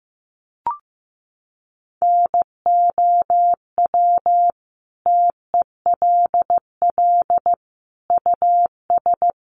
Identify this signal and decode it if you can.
Morse code, heard as a keyed tone, about 15 wpm, 700 Hz. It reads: NOW TELL US
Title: Sentences from Top 600 Words 15wpm